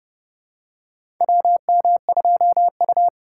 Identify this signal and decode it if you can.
Morse code, heard as a keyed tone, about 30 wpm, 700 Hz. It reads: WM2U
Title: USA Call Signs - 4 Characters - 2x Character Spacing - 30wpm